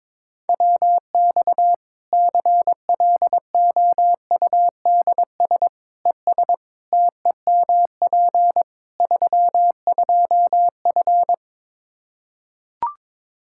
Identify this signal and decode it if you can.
Morse code, heard as a keyed tone, about 23 wpm, 700 Hz. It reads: WX CLOUDS ES TEMP 32F
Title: Practice QSO - Exchange 2 - Info - 22wpm